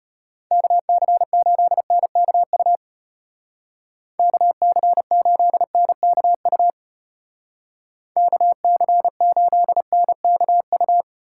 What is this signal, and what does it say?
Morse code, heard as a keyed tone, about 32 wpm, 700 Hz. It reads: KC8NKU KC8NKU KC8NKU